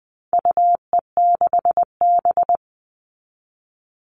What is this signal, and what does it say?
Morse code, heard as a keyed tone, about 20 wpm, 700 Hz. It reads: UE6B
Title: All Cut Numbers - 4-Digit 20wpm